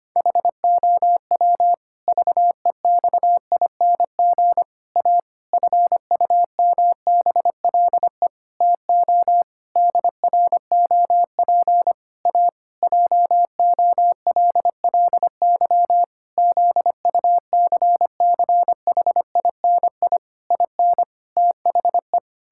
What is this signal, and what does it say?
Morse code, heard as a keyed tone, about 25 wpm, 700 Hz. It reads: HOW VEXING A FUMBLE TO DROP A JOLLY ZUCCHINI IN THE
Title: Pangram Challenge v2 - 25wpm